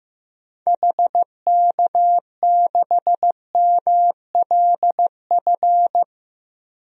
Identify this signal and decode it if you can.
Morse code, heard as a keyed tone, about 16 wpm, 700 Hz. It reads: HK6MLF
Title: DXCC-100 Call Signs - 6 Characters - Repeated 2x - 15wpm